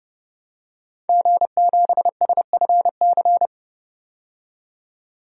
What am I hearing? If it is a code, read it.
Morse code, heard as a keyed tone, about 30 wpm, 700 Hz. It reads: G7SFC
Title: DXCC-100 Call Signs - 5 Characters - 30wpm